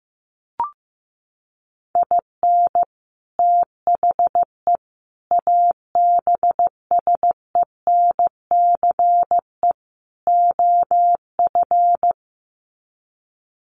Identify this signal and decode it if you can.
Morse code, heard as a keyed tone, about 15 wpm, 700 Hz. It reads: IN THE ABSENCE OF
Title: Sets of 4 Words 15wpm